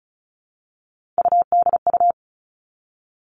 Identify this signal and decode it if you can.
Morse code, heard as a keyed tone, about 36 wpm, 700 Hz. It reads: UDU
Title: Three Challenging Letter-Number Combinations 35wpm